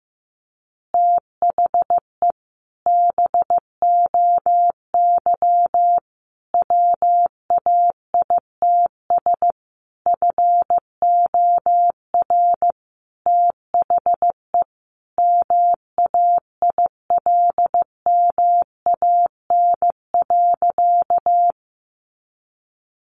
Morse code, heard as a keyed tone, about 14 wpm, 700 Hz. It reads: THE BOY WAITS FOR THE MAILMAN.
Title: Easy Phrases 15wpm